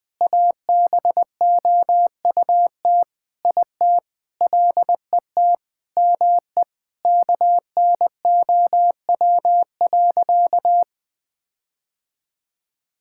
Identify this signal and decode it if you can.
Morse code, heard as a keyed tone, about 20 wpm, 700 Hz. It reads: ABOUT IT LET ME KNOW.
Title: Sentences with 8-Word Phrases - 20wpm